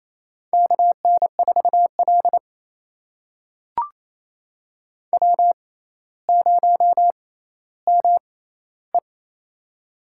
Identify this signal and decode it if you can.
Morse code, heard as a keyed tone, about 28 wpm, spaced as for 14 wpm, 700 Hz. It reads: KN4L W0ME
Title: USA Call Signs - 4 Characters - 6x Character Spacing - 28wpm